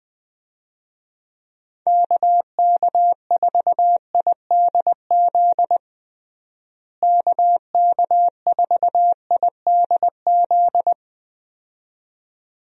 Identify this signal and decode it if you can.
Morse code, heard as a keyed tone, about 20 wpm, 700 Hz. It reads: KK4IDZ KK4IDZ
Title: USA Call Signs - 6 Characters - Repeated 2x - 20wpm